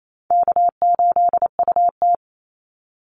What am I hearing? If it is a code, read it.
Morse code, heard as a keyed tone, about 28 wpm, 700 Hz. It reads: K8UT
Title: Easy Call Signs Repeated 3x - 28wpm